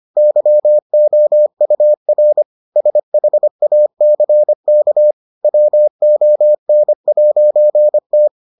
Morse code, heard as a keyed tone, about 25 wpm, 600 Hz. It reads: YOUR SHACK WON'T